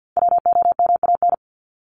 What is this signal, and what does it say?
Morse code, heard as a keyed tone, about 50 wpm, 700 Hz. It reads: WOMAN
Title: Top 1001-1250 Words 50wpm